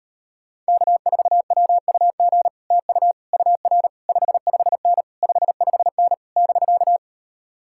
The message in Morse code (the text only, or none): K4WUG TU UR 55N 55N <BK>